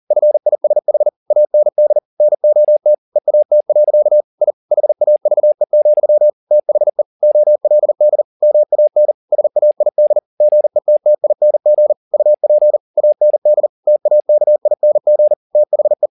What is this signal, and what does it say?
Morse code, heard as a keyed tone, about 40 wpm, 600 Hz. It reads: FISH AND NOT EAT. I HAVE, THE OLD MAN SAID GETTING UP AND TAKING THE